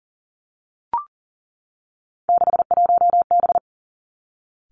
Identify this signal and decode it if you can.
Morse code, heard as a keyed tone, about 38 wpm, 700 Hz. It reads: 61B
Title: Three Challenging Letter-Number Combinations 40wpm